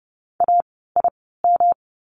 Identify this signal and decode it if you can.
Morse code, heard as a keyed tone, about 30 wpm, spaced as for 17 wpm, 700 Hz. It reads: AIM
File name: course / 30:10wpm / 108 Lesson 18f - Letter m - Words 30wpm